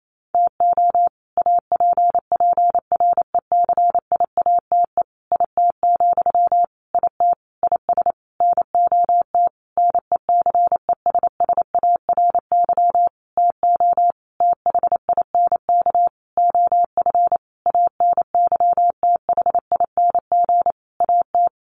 Morse code, heard as a keyed tone, about 28 wpm, 700 Hz. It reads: TO APPRECIATE IT, IT IS NOT NECESSARY TO THINK OF ANYTHING AT
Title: Sentences from Top 1300 Words 28wpm